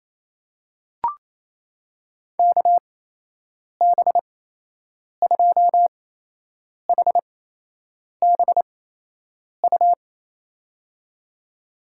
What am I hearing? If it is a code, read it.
Morse code, heard as a keyed tone, about 28 wpm, spaced as for 8 wpm, 700 Hz. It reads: KB2HBU
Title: USA Call Signs - 6 Characters - 8x Character Spacing - 28wpm